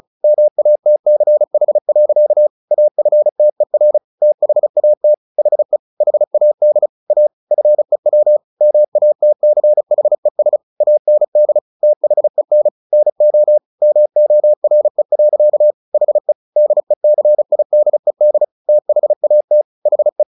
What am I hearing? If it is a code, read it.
Morse code, heard as a keyed tone, about 35 wpm, 600 Hz. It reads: MATCH. AFTER THAT HE HAD A FEW MATCHES AND THEN NO MORE. HE DECIDED THAT HE
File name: the old man and the sea 2 35wpm